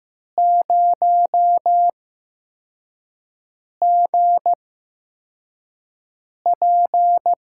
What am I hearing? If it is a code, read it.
Morse code, heard as a keyed tone, about 15 wpm, spaced as for 4 wpm, 700 Hz. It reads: 0GP
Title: USA Call Signs - 5 Characters - 8x Character Spacing - 15wpm